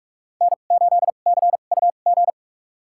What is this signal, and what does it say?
Morse code, heard as a keyed tone, about 41 wpm, spaced as for 31 wpm, 700 Hz. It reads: N8CUG
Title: USA Call Signs - 2x Character Spacing - 45wpm